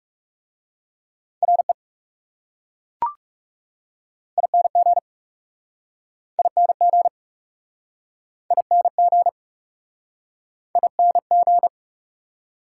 Morse code, heard as a keyed tone, about 37 wpm, 700 Hz. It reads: RE ING ING ING ING
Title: Speed-Racing Prefixes and Suffixes 30wpm